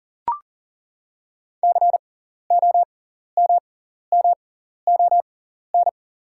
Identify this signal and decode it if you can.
Morse code, heard as a keyed tone, about 40 wpm, spaced as for 14 wpm, 690 Hz. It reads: COMMON
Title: 6x Character Spacing - Top 500 Words - 40wpm